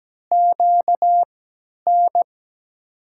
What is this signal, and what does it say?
Morse code, heard as a keyed tone, about 17 wpm, spaced as for 10 wpm, 700 Hz. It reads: QN
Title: USA Call Signs - 6 Characters - 3x Character Spacing - 17wpm